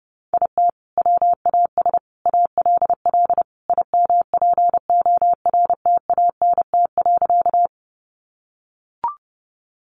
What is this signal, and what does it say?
Morse code, heard as a keyed tone, about 30 wpm, 700 Hz. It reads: IT WAS ALL IMPORTANT.
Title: Sentences from Top 600 Words 30wpm